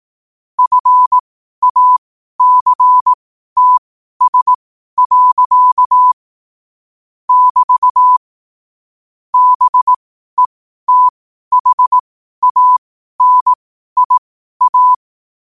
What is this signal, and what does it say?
Morse code, heard as a keyed tone, about 18 wpm, spaced as for 12 wpm, 1000 Hz. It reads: FACTS. <BT> BETHANIA
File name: 12 wpm